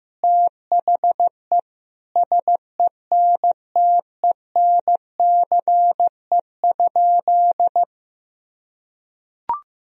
Morse code, heard as a keyed tone, about 15 wpm, 700 Hz. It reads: THE SENTENCE?
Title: Sentences from Top 1000 Words - Review Part 1 - 15wpm